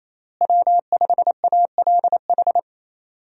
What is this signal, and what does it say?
Morse code, heard as a keyed tone, about 27 wpm, 700 Hz. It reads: W5ALH
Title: USA Call Signs - 5 Characters - Repeated 3x - 28wpm